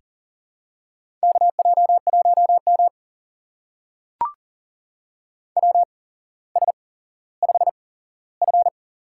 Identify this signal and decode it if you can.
Morse code, heard as a keyed tone, about 40 wpm, spaced as for 17 wpm, 690 Hz. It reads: KJ1M WS5F
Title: USA Call Signs - 4 Characters - 8x Character Spacing - 40wpm